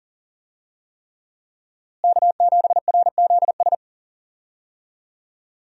Morse code, heard as a keyed tone, about 40 wpm, 700 Hz. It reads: K7RZS
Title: USA Call Signs - 5 Characters - Repeated 3x - 40wpm